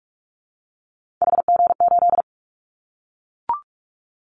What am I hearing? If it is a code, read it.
Morse code, heard as a keyed tone, about 44 wpm, 700 Hz. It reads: HG8
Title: Callsign Prefixes for Top 25 DXCC - Repeated 3x - 45wpm